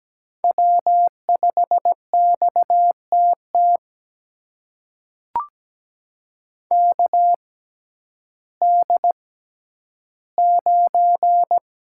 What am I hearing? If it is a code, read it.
Morse code, heard as a keyed tone, about 17 wpm, spaced as for 10 wpm, 700 Hz. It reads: W5XTT KD9
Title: USA Call Signs - 6x Character Spacing - 17wpm